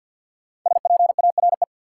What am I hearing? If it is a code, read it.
Morse code, heard as a keyed tone, about 52 wpm, 700 Hz. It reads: SPARE
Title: Five Letter Words - 50wpm